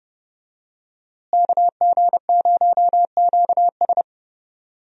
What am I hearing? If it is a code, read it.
Morse code, heard as a keyed tone, about 30 wpm, 700 Hz. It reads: KG0QS
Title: USA Call Signs - 5 Characters - Repeated 2x - 30wpm